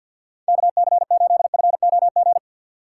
Morse code, heard as a keyed tone, about 48 wpm, 700 Hz. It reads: KC8FOG
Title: USA Call Signs - 6 Characters - 4x Character Spacing - 50wpm